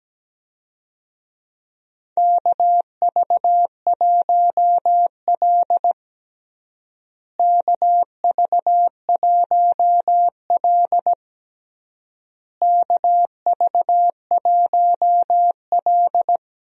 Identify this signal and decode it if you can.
Morse code, heard as a keyed tone, about 17 wpm, 700 Hz. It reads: KV1L KV1L KV1L
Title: USA Call Signs - 4 Characters - Repeated 3x - 17wpm